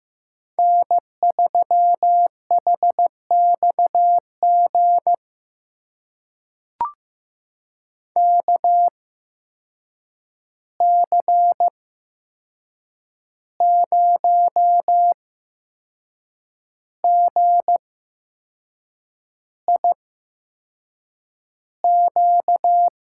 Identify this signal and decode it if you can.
Morse code, heard as a keyed tone, about 15 wpm, spaced as for 6 wpm, 700 Hz. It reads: N3HXG KC0GIQ
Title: USA Call Signs - 8x Character Spacing - 15wpm